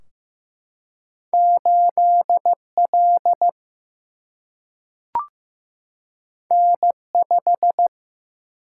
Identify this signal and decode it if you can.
Morse code, heard as a keyed tone, about 15 wpm, 700 Hz. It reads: 8L N5
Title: Single-Letter Single-Number 15wpm